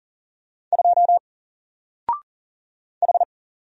Morse code, heard as a keyed tone, about 40 wpm, 700 Hz. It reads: 2 H
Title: Single Letter-Number 40wpm